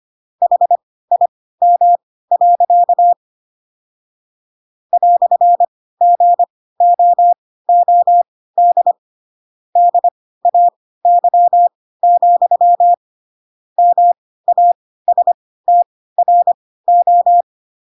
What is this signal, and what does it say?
Morse code, heard as a keyed tone, about 25 wpm, spaced as for 16 wpm, 700 Hz. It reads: HIM. "GOOD DAY, MASTRO
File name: Pinocchio / Pinocchio ch2 10 25 ik1hge